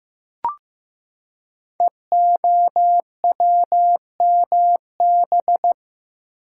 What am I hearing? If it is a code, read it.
Morse code, heard as a keyed tone, about 15 wpm, 700 Hz. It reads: EOWMB
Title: Code Groups - 5 Letter:Numbers - 15wpm